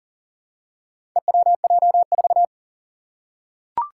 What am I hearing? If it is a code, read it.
Morse code, heard as a keyed tone, about 40 wpm, 700 Hz. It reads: EWJ4
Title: Four Challenging Letter-Number Combinations 40wpm